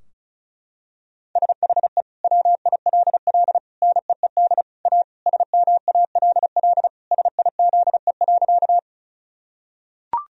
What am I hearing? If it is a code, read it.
Morse code, heard as a keyed tone, about 34 wpm, 700 Hz. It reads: SHE WILL NEED A SMALL SIZE.
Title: Sentences from Top 1000 Words - Review Part 2 - 35wpm